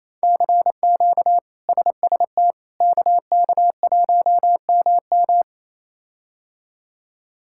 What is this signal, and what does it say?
Morse code, heard as a keyed tone, about 27 wpm, 700 Hz. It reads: CQ SST KK1MM
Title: K1USN SST QSO Practice - 28wpm